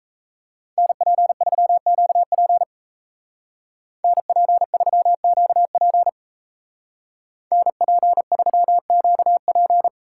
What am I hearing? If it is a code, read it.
Morse code, heard as a keyed tone, about 34 wpm, 700 Hz. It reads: NP3QP NP3QP NP3QP
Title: Speed Racing - USA 2x2 Call Signs 28wpm